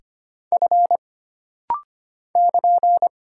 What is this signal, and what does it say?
Morse code, heard as a keyed tone, about 25 wpm, 700 Hz. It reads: F <KN>
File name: Single Character (Letter:Number:Punctuation:Prosign) - Rapid-Fire - 25wpm